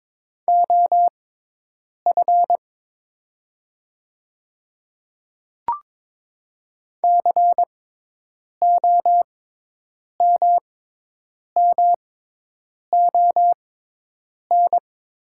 Morse code, heard as a keyed tone, about 22 wpm, spaced as for 8 wpm, 700 Hz. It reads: OF COMMON
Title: 6x Character Spacing - Top 500 Words - 22wpm